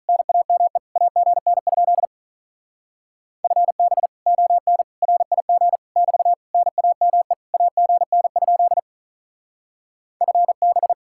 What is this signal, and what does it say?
Morse code, heard as a keyed tone, about 43 wpm, 700 Hz. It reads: NAME AGN? FB ON RIG <BT> NAME AGN? FB
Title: Speed-Racing Common QSO Phrases - 35wpm